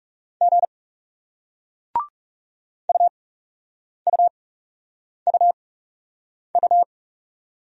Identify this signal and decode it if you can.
Morse code, heard as a keyed tone, about 40 wpm, spaced as for 8 wpm, 700 Hz. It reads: G UUUU